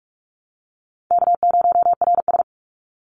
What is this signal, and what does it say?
Morse code, heard as a keyed tone, about 46 wpm, 700 Hz. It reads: K0RS
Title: NA SOTA Chaser Callsigns 45wpm